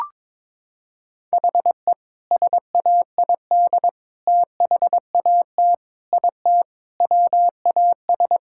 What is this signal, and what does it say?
Morse code, heard as a keyed tone, about 22 wpm, 700 Hz. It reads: HE SAID THAT IT WAS
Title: Sentences from Top 1600 Words 22wpm